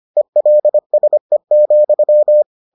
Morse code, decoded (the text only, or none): ELSE,